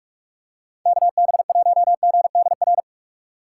Morse code, heard as a keyed tone, about 47 wpm, 700 Hz. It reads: KB1GDR
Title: USA Call Signs - 6 Characters - 4x Character Spacing - 45wpm